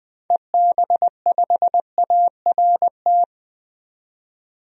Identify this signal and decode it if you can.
Morse code, heard as a keyed tone, about 20 wpm, 700 Hz. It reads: EB5ART
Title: DXCC-100 Call Signs - 6 Characters - 20wpm